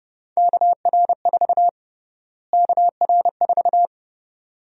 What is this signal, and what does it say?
Morse code, heard as a keyed tone, about 30 wpm, 700 Hz. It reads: KR4 KR4
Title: USA Call Sign Prefixes - Repeated 2x - 30wpm